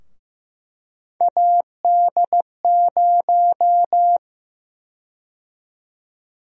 Morse code, heard as a keyed tone, about 15 wpm, 700 Hz. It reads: AD0
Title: USA Call Sign Prefixes - Repeated 2x - 15wpm